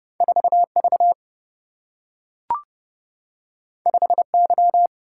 Morse code, heard as a keyed tone, about 30 wpm, 700 Hz. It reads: 4 V 5 Y
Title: Single-Letter Single-Number 30wpm